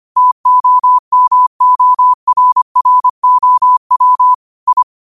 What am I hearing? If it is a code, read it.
Morse code, heard as a keyed tone, about 25 wpm, 1000 Hz. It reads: TOMORROW I
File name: qotdfiles / 25wpm